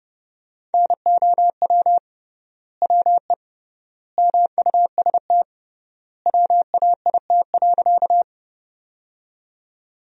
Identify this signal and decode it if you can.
Morse code, heard as a keyed tone, about 30 wpm, 700 Hz. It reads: NOW WE MUST WAIT.
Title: 3x Word-Spacing - Sentences from Top 500 Words - 30wpm